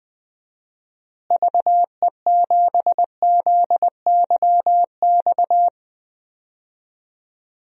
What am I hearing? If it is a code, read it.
Morse code, heard as a keyed tone, about 19 wpm, 700 Hz. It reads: VE7ZYX
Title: DXCC-100 Call Signs - 6 Characters - Repeated 2x - 20wpm